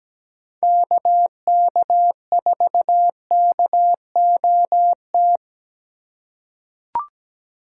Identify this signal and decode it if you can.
Morse code, heard as a keyed tone, about 17 wpm, 700 Hz. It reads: KK4KOT
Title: USA Call Signs - 6 Characters - 2x Character Spacing - 17wpm